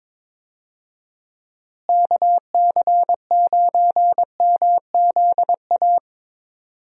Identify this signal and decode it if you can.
Morse code, heard as a keyed tone, about 22 wpm, 700 Hz. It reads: KC9MZA
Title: USA Call Signs - 6 Characters - Repeated 3x - 22wpm